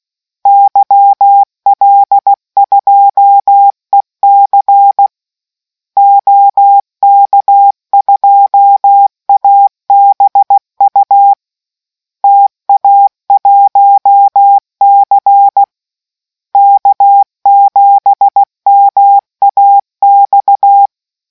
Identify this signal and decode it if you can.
Morse code, heard as a keyed tone, about 16 wpm, 800 Hz. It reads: YL2EC OK2ABU TA1C K7MAX